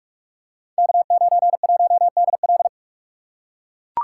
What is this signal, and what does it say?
Morse code, heard as a keyed tone, about 44 wpm, 690 Hz. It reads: K91DL
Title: Code Groups - 5 Letter:Numbers - 45wpm